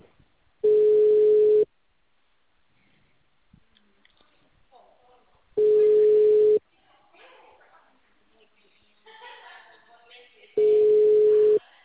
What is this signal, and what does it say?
Ringback tone (Europe, 425 Hz), with speech.
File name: out- -2018- - - .75686